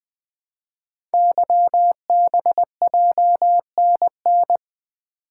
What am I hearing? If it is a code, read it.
Morse code, heard as a keyed tone, about 20 wpm, 700 Hz. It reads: YBJNN
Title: Code Groups - 5 Letters - 20wpm